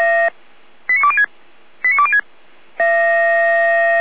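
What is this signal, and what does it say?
Five-tone selective call (ZVEI), heard as quick runs of five tones, tones of about 80 ms. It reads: 89206 89206
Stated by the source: ZVEI five-tone selective call received on SDR.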